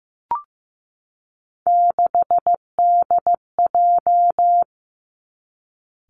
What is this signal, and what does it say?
Morse code, heard as a keyed tone, about 15 wpm, 700 Hz. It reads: 6DJ
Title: Three Challenging Letter-Number Combinations 15wpm